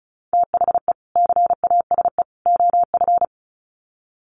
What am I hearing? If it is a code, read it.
Morse code, heard as a keyed tone, about 34 wpm, 700 Hz. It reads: THE CASE OF
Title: Sets of 3 Words 35wpm